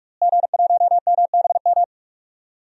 Morse code, heard as a keyed tone, about 46 wpm, 700 Hz. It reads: G1MBM
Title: DXCC-100 Call Signs - 5 Characters - Repeated 2x - 45wpm